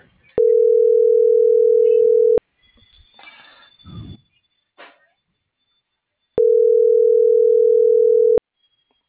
Ringback tone (North America), 440 + 480 Hz.